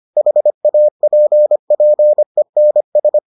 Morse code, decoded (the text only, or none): HAPPENS